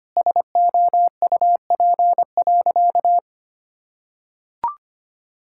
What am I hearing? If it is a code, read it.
Morse code, heard as a keyed tone, about 25 wpm, 700 Hz. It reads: SOUP.
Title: Sentences from Top 1000 Words - Review Part 6 - 25wpm